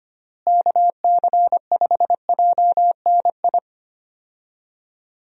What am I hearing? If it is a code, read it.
Morse code, heard as a keyed tone, about 25 wpm, 700 Hz. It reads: KC5JNI
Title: USA Call Signs - 6 Characters - Repeated 3x - 25wpm